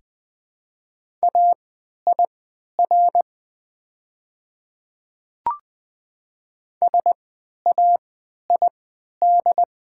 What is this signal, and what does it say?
Morse code, heard as a keyed tone, about 20 wpm, spaced as for 11 wpm, 690 Hz. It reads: AIR SAID
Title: 3x Character Spacing - Top 200 Words - 20wpm